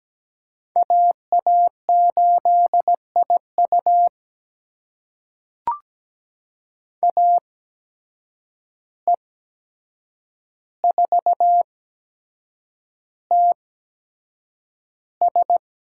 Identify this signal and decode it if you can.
Morse code, heard as a keyed tone, about 17 wpm, spaced as for 7 wpm, 700 Hz. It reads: AA8IU AE4TS
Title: USA Call Signs - 5 Characters - 8x Character Spacing - 17wpm